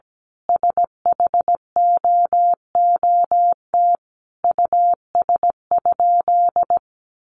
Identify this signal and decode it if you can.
Morse code, heard as a keyed tone, about 17 wpm, 700 Hz. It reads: SHOOT US?
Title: Sentences from Top 1200 Words 17wpm